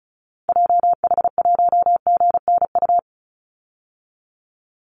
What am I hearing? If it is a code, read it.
Morse code, heard as a keyed tone, about 34 wpm, 700 Hz. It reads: JH1GNU
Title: Call Signs Repeated 2x - 35wpm